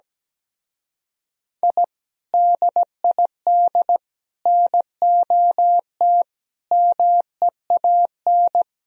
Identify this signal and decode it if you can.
Morse code, heard as a keyed tone, about 18 wpm, 700 Hz. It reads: I DID NOT MEAN
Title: Speed-Racing - Sets of 5 Words 17wpm